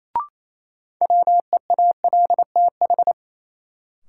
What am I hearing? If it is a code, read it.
Morse code, heard as a keyed tone, about 28 wpm, 700 Hz. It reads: WEALTH
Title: Sentences from Top 1200 Words - New Word Drill Alt - 28wpm